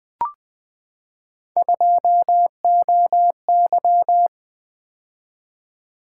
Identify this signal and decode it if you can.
Morse code, heard as a keyed tone, about 20 wpm, 700 Hz. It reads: 2OY
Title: Three Letter-Number Combinations - 20wpm